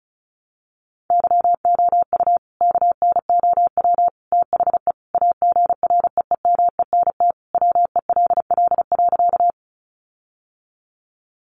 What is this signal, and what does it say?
Morse code, heard as a keyed tone, about 34 wpm, 700 Hz. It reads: YOU KNOW THE AGREEMENT WELL.
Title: Sentences from Top 1500 Words 35wpm